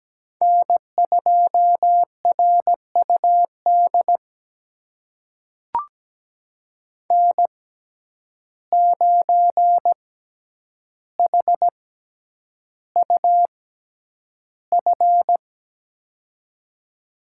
Morse code, heard as a keyed tone, about 17 wpm, spaced as for 9 wpm, 700 Hz. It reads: N2RUD N9HUF
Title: USA Call Signs - 5 Characters - 6x Character Spacing - 17wpm